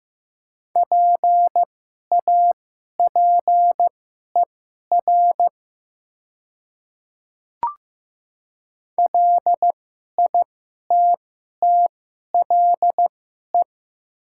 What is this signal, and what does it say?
Morse code, heard as a keyed tone, about 15 wpm, spaced as for 11 wpm, 700 Hz. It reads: PAPER LITTLE
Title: 2x Character Spacing - Top 300 Words - 15wpm